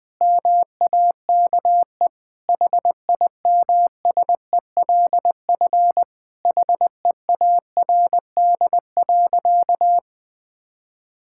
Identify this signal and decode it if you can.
Morse code, heard as a keyed tone, about 20 wpm, 700 Hz. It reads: MAKE HIMSELF HEARD.
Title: Sentences from Top 1600 Words 20wpm